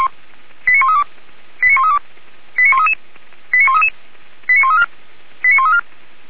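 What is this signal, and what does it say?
Five-tone selective call (ZVEI), heard as quick runs of five tones, tones of about 70 ms. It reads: 89132 89132 89133 89133 89134 89134
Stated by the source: ZVEI five-tone selective call received on an SDR.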